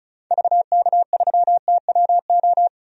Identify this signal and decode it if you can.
Morse code, heard as a keyed tone, about 34 wpm, 700 Hz. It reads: VK3TWO
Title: DXCC-100 Call Signs - 6 Characters - 35wpm